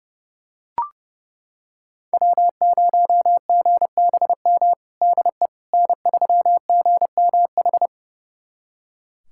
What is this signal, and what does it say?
Morse code, heard as a keyed tone, about 30 wpm, 700 Hz. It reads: W0GBM DE N3GMH
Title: USA Callsign DE Callsign - 5 Characters - 30wpm